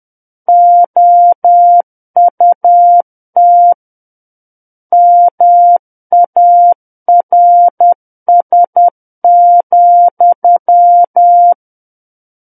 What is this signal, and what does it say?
Morse code, heard as a keyed tone, about 10 wpm, 700 Hz. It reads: OUT MARS,
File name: Book One / 01-Chapter 1 - 10 WPM